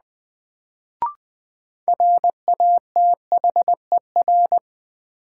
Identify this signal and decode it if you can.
Morse code, heard as a keyed tone, about 21 wpm, 700 Hz. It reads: RATHER
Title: Sentences from Top 700 Words - New Word Drill - 20wpm